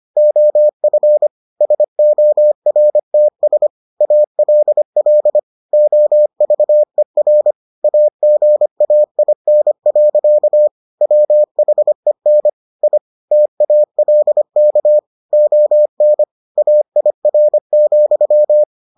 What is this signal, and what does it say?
Morse code, heard as a keyed tone, about 25 wpm, 600 Hz. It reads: OF SORTS ALL OVER AGAIN. WHEN I TALK ON AIR,